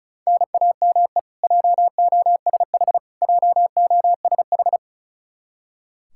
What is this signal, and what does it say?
Morse code, heard as a keyed tone, about 34 wpm, 700 Hz. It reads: NAME JOSH JOSH
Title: Practice QSO - Exchange 1 - Beginning - 35wpm